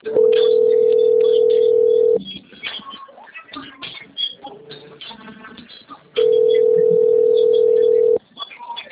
Ringback tone (North America), 440 + 480 Hz.